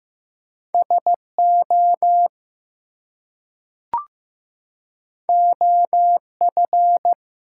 Morse code, heard as a keyed tone, about 15 wpm, 700 Hz. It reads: SO OF
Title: Two Letter Words - 15wpm